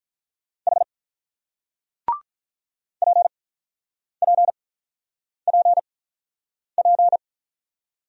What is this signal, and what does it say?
Morse code, heard as a keyed tone, about 47 wpm, spaced as for 9 wpm, 700 Hz. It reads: H PPPP